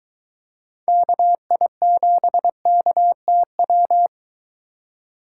Morse code, heard as a keyed tone, about 23 wpm, 700 Hz. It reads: KI7KTW